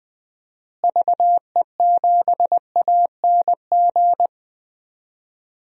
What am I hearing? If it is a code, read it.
Morse code, heard as a keyed tone, about 20 wpm, 700 Hz. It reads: VE7ANG
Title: DXCC-100 Call Signs - 6 Characters - Repeated 3x - 20wpm